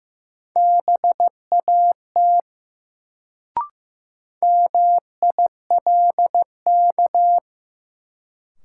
Morse code, heard as a keyed tone, about 15 wpm, 700 Hz. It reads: BAT MILK
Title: Sentences from Top 1500 Words - New Word Drill Alt - 15wpm